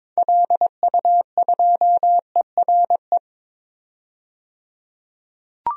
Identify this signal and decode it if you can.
Morse code, heard as a keyed tone, about 22 wpm, 700 Hz. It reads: LU2ERE
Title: DXCC-100 Call Signs - 6 Characters - 22wpm